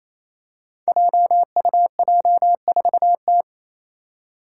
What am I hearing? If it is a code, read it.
Morse code, heard as a keyed tone, about 28 wpm, 700 Hz. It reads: JUJ4T
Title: Five Challenging Letter-Number Combinations 28wpm